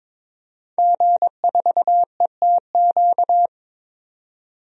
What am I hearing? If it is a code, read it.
Morse code, heard as a keyed tone, about 22 wpm, 700 Hz. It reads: G4ETQ
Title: DXCC-100 Call Signs - 5 Characters - 22wpm